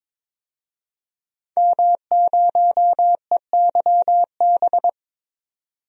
Morse code, heard as a keyed tone, about 22 wpm, 700 Hz. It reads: M0EYB
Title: DXCC-100 Call Signs - 5 Characters - 22wpm